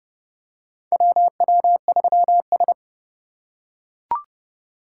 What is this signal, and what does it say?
Morse code, heard as a keyed tone, about 30 wpm, 700 Hz. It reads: WW3S
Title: USA Call Signs - 4 Characters - 3x Character Spacing - 30wpm